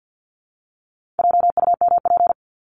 Morse code, heard as a keyed tone, about 50 wpm, 700 Hz. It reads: JUMP
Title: Top 1251-1500 Words 50wpm